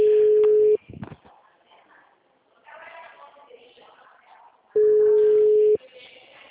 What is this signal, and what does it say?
Ringback tone (Europe, 425 Hz), with speech.
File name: out- -2048- - - .39082